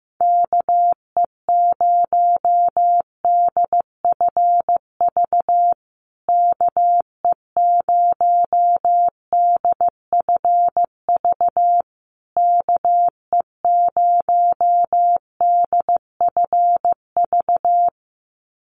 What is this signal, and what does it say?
Morse code, heard as a keyed tone, about 15 wpm, 700 Hz. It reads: KE0DFV KE0DFV KE0DFV
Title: USA Call Signs Repeated 3x 15wpm